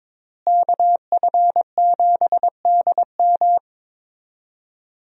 Morse code, heard as a keyed tone, about 21 wpm, 700 Hz. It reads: KF7DM
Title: USA Call Signs - 5 Characters - Repeated 3x - 22wpm